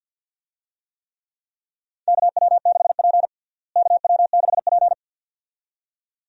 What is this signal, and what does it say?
Morse code, heard as a keyed tone, about 46 wpm, 700 Hz. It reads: KW6P KW6P
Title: USA Call Signs - 4 Characters - Repeated 2x - 50wpm